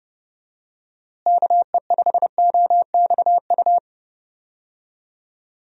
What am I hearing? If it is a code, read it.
Morse code, heard as a keyed tone, about 32 wpm, 700 Hz. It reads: KE5OXU
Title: USA Call Signs - 6 Characters - Repeated 3x - 30wpm